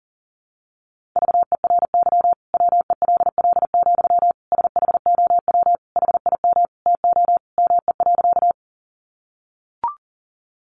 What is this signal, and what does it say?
Morse code, heard as a keyed tone, about 40 wpm, 700 Hz. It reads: VERY WELL, SHOW HIM TO ME.
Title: Sentences from Top 200 Words 40wpm